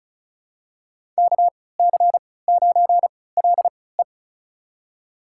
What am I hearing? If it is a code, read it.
Morse code, heard as a keyed tone, about 36 wpm, spaced as for 20 wpm, 700 Hz. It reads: KC9LE
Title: USA Call Signs - 5 Characters - 3x Character Spacing - 35wpm